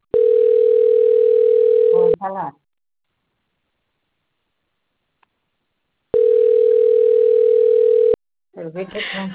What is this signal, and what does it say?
Ringback tone (North America), 440 + 480 Hz, with speech.
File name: out- - - - - .85494